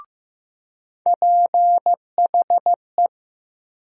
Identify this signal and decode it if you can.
Morse code, heard as a keyed tone, about 15 wpm, 700 Hz. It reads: PHE
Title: USA Call Sign Postfixes - 3 Letters 15wpm